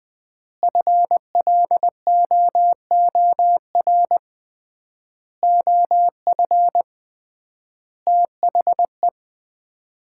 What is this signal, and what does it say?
Morse code, heard as a keyed tone, about 20 wpm, 700 Hz. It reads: FLOOR OF THE
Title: 3x Word-Spacing - Sets of 5 Words - 20wpm